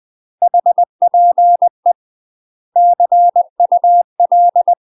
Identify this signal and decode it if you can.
Morse code, heard as a keyed tone, about 20 wpm, 700 Hz. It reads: HPE CUL
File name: UK MASTER QSOs TRAINING / 20WPM / UK MASTER QSOs 0063